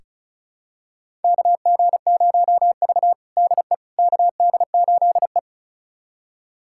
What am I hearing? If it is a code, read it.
Morse code, heard as a keyed tone, about 36 wpm, 700 Hz. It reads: KG0V DE KD8E
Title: USA Callsign DE Callsign - 4 Characters - 35wpm